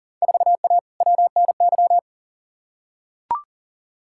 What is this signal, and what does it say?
Morse code, heard as a keyed tone, about 37 wpm, 700 Hz. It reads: 4A WNY
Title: Field Day Exchanges Encore - 40wpm